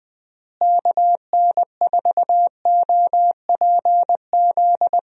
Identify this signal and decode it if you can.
Morse code, heard as a keyed tone, about 21 wpm, 700 Hz. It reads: KN4OPZ
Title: USA Call Signs - 6 Characters - Repeated 2x - 20wpm